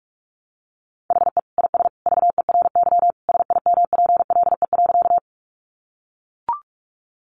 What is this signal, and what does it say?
Morse code, heard as a keyed tone, about 44 wpm, 700 Hz. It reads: HE IS VERY SIMPLE.
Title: Sentences from Top 400 Words 45wpm